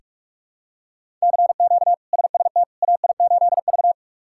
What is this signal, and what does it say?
Morse code, heard as a keyed tone, about 42 wpm, 700 Hz. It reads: CQ SST AI8V
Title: K1USN SST QSO Practice - 45wpm